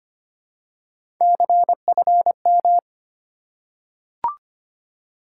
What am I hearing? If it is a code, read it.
Morse code, heard as a keyed tone, about 25 wpm, 700 Hz. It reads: CFM
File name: Top 90 QSO Elements - 4x Character Spacing - 25wpm